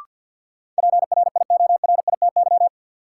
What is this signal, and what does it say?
Morse code, heard as a keyed tone, about 52 wpm, 700 Hz. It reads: PRIORITY
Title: Sentences from Top 1600 Words - New Word Drill Alt - 50wpm